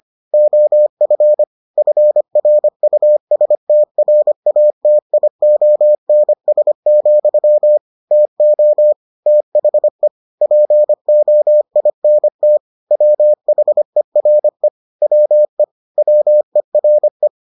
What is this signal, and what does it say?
Morse code, heard as a keyed tone, about 25 wpm, 600 Hz. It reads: OF FRUSTRATIONS, TO THE POINT WHERE WE WERE